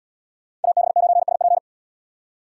Morse code, heard as a keyed tone, about 72 wpm, 700 Hz. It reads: AB9SP